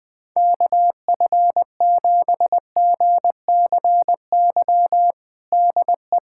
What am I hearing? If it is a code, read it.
Morse code, heard as a keyed tone, about 21 wpm, 700 Hz. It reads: KF7GCY DE
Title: USA Callsign DE Callsign - 6 Characters - 20wpm